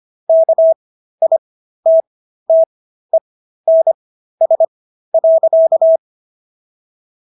Morse code, heard as a keyed tone, about 25 wpm, spaced as for 13 wpm, 650 Hz. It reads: KITTENS.